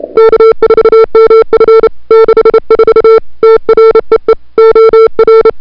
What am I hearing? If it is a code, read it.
Morse code, heard as a keyed tone, about 31 wpm, 440 Hz. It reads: K4MF 64 TREE OR
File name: n6tr / Fab2007CWSprintHour1